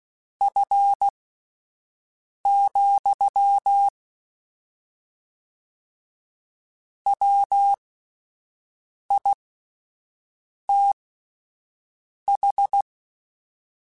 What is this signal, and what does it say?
Morse code, heard as a keyed tone, about 16 wpm, spaced as for 5 wpm, 800 Hz. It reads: F, WITH
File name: Spaced Characters / Spaced Characters-mp3s / Weather sp @10